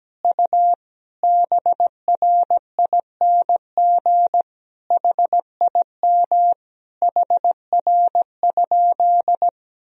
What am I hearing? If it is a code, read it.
Morse code, heard as a keyed tone, about 17 wpm, 700 Hz. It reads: U BRING HIM HR?
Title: Sentences from Top 500 Words and Abbreviations 17wpm